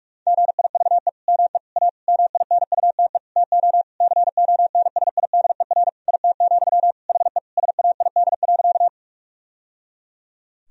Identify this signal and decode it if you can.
Morse code, heard as a keyed tone, about 41 wpm, 700 Hz. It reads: GIVE ME A MINUTE TO CONSIDER IT, HE SAID.
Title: Sentences from Top 1000 Words - Review Part 4 - 45wpm